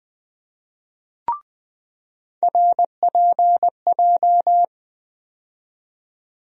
Morse code, heard as a keyed tone, about 21 wpm, 700 Hz. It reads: RPJ
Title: USA Call Sign Postfixes - 3 Letters 20wpm